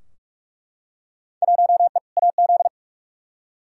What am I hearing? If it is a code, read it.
Morse code, heard as a keyed tone, about 47 wpm, 700 Hz. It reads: 1E AZ
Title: Field Day Exchanges - 45wpm